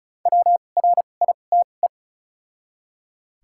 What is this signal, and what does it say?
Morse code, heard as a keyed tone, about 34 wpm, spaced as for 25 wpm, 700 Hz. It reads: WRITE
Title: 2x Character Spacing - Top 300 Words - 35wpm